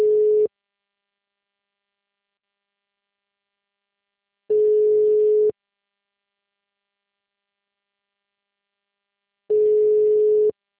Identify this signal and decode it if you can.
Ringback tone (Europe, 425 Hz).